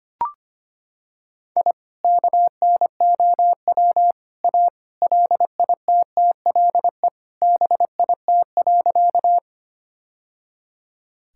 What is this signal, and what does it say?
Morse code, heard as a keyed tone, about 25 wpm, 700 Hz. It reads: I KNOW A LITTLE BIT.
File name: practice / 25wpm / Sentences with 3-Word Phrases - 25wpm